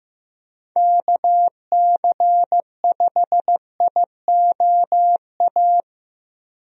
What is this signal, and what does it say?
Morse code, heard as a keyed tone, about 15 wpm, 700 Hz. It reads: KC5IOA
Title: USA Call Signs - 6 Characters - Repeated 3x - 15wpm